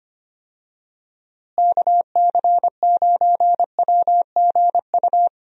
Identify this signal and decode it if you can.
Morse code, heard as a keyed tone, about 25 wpm, 700 Hz. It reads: KC9WGU
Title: USA Call Signs - 6 Characters - Repeated 3x - 25wpm